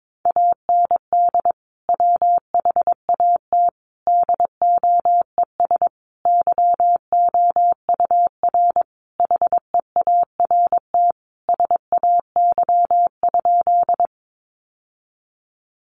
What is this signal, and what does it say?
Morse code, heard as a keyed tone, about 22 wpm, 700 Hz. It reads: AND WHAT DOES YOUR HEART SAY?
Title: Sentences from Top 1200 Words 22wpm